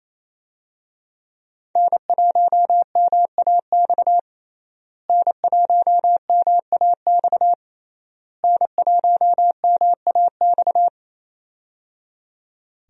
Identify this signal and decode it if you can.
Morse code, heard as a keyed tone, about 27 wpm, 700 Hz. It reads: N1MAX N1MAX N1MAX
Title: USA Call Signs - 5 Characters - Repeated 3x - 28wpm